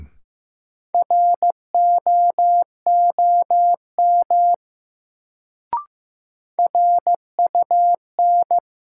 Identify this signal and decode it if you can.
Morse code, heard as a keyed tone, about 15 wpm, 700 Hz. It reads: ROOM RUN
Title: Sets of 4 Words - New Word Drill - 15wpm